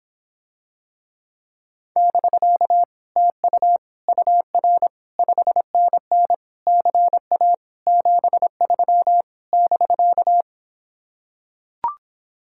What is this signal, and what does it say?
Morse code, heard as a keyed tone, about 26 wpm, 700 Hz. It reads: <BK> TU UR 5NN CA 73 <BK>